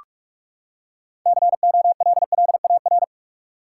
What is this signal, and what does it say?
Morse code, heard as a keyed tone, about 46 wpm, 700 Hz. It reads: COLLAR
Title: Six Letter Words - 45wpm